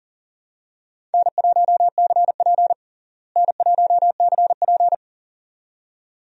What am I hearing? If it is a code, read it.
Morse code, heard as a keyed tone, about 40 wpm, 700 Hz. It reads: N1CP N1CP
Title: USA Call Signs - 4 Characters - Repeated 2x - 40wpm